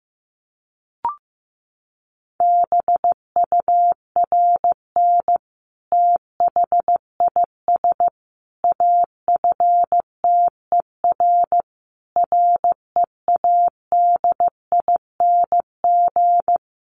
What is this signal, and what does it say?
Morse code, heard as a keyed tone, about 15 wpm, 700 Hz. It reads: BURN THIS AFTER READING
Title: Sentences from Top 1300 Words 15wpm